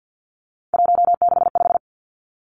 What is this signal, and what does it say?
Morse code, heard as a keyed tone, about 50 wpm, 700 Hz. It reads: 165
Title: Three Challenging Letter-Number Combinations 50wpm